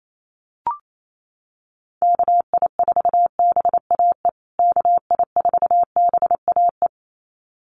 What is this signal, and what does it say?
Morse code, heard as a keyed tone, about 28 wpm, 700 Hz. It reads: KI4BAE KI4BAE
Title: USA Call Signs Repeated 2x 28wpm